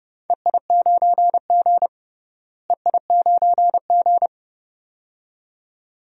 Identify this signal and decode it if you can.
Morse code, heard as a keyed tone, about 32 wpm, 700 Hz. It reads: EI9G EI9G
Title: DXCC-100 Call Signs - 4 Characters - Repeated 2x - 30wpm